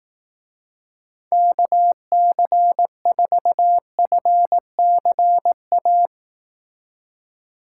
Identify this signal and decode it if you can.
Morse code, heard as a keyed tone, about 18 wpm, 700 Hz. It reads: KC4FCA
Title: Speed Racing - USA 2x3 Call Signs 15wpm